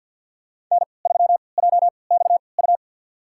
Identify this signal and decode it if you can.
Morse code, heard as a keyed tone, about 48 wpm, spaced as for 28 wpm, 700 Hz. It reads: N3JXU
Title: USA Call Signs - 3x Character Spacing - 50wpm